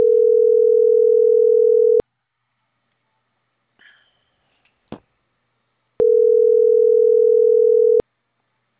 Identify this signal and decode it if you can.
Ringback tone (North America), 440 + 480 Hz.